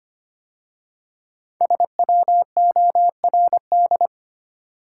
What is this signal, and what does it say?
Morse code, heard as a keyed tone, about 25 wpm, 700 Hz. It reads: SWORD
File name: 5 Letter Words - First 3 Letters Given - 25wpm